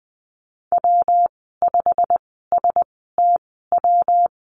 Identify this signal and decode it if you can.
Morse code, heard as a keyed tone, about 20 wpm, spaced as for 15 wpm, 700 Hz. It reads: W5STW
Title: 161 Lesson 26g - Letter g - Call Signs 20wpm